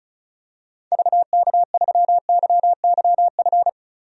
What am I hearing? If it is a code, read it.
Morse code, heard as a keyed tone, about 34 wpm, 700 Hz. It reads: VK3YYF
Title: DXCC-100 Call Signs - 6 Characters - 35wpm